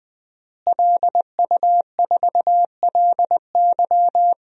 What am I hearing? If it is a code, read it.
Morse code, heard as a keyed tone, about 20 wpm, 700 Hz. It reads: LU4LY
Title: DXCC-100 Call Signs - 5 Characters - Repeated 3x - 20wpm